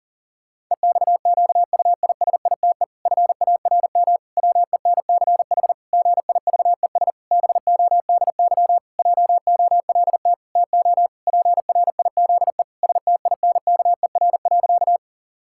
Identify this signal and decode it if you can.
Morse code, heard as a keyed tone, about 37 wpm, 700 Hz. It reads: EXQUISITE FARM WENCH GIVES BODY JOLT TO PRIZE STINKER.
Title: Pangram Challenge v2 - 40wpm